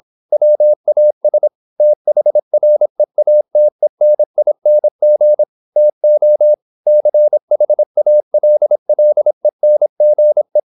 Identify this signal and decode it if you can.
Morse code, heard as a keyed tone, about 26 wpm, 600 Hz. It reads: WAS THREATENING TO CHALLENGE
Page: Never 26wpm | CW med Gnister